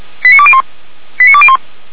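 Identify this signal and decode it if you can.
Five-tone selective call (ZVEI), heard as quick runs of five tones, tones of about 70 ms. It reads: 89201 89201
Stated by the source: ZVEI five-tone selective call received by SDR.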